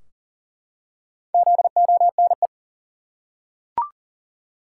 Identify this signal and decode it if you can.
Morse code, heard as a keyed tone, about 38 wpm, 700 Hz. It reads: ZONE
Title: Four Letter Words - 40wpm